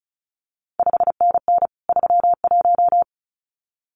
Morse code, heard as a keyed tone, about 36 wpm, 700 Hz. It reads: 5NN 31
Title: CQ WW DX Contest Exchange - 35wpm